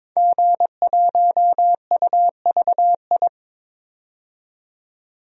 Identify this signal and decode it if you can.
Morse code, heard as a keyed tone, about 23 wpm, 700 Hz. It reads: G1UVI
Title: DXCC-100 Call Signs - 5 Characters - 22wpm